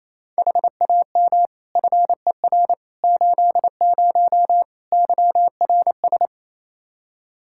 Morse code, heard as a keyed tone, about 28 wpm, 700 Hz. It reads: HAM FER 80 YRS
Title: Practice QSO - Exchange 3 - More Info - 28wpm